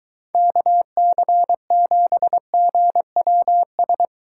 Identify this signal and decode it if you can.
Morse code, heard as a keyed tone, about 23 wpm, 700 Hz. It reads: KC7GWS